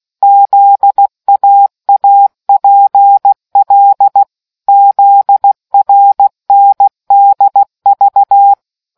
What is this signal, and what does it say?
Morse code, heard as a keyed tone, about 16 wpm, 800 Hz. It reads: ZAAPL ZRNDV